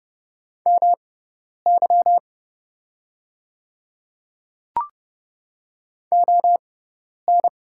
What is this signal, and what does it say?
Morse code, heard as a keyed tone, about 30 wpm, spaced as for 10 wpm, 700 Hz. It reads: MY ON
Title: 6x Character Spacing - Top 200 Words - 30wpm